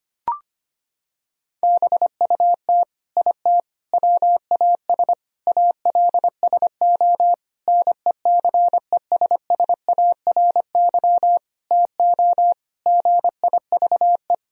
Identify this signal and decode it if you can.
Morse code, heard as a keyed tone, about 25 wpm, 700 Hz. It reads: BUT IT WAS ALSO NECESSARY TO GIVE
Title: Sentences from Top 1000 Words - Review Part 2 - 25wpm